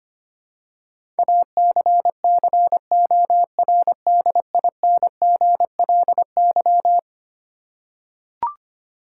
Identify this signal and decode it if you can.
Morse code, heard as a keyed tone, about 25 wpm, 700 Hz. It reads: ACCORDINGLY
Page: morse-code-ninja-practice-25wpm